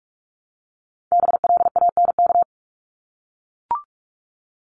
Morse code, heard as a keyed tone, about 44 wpm, 700 Hz. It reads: BLANK
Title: Top 1251-1500 Words 45wpm